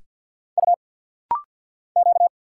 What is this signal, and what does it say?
Morse code, heard as a keyed tone, about 50 wpm, 700 Hz. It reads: U Q
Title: Single Letters - Rapid-Fire 50wpm